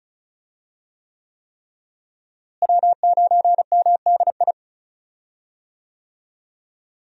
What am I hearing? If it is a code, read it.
Morse code, heard as a keyed tone, about 34 wpm, 700 Hz. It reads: W9MDI
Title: USA Call Signs - 5 Characters - Repeated 3x - 35wpm